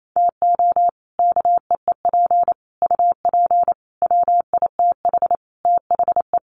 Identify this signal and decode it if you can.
Morse code, heard as a keyed tone, about 28 wpm, 700 Hz. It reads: TO KEEP UP WITH THE
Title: Sets of 5 Words Encore 28wpm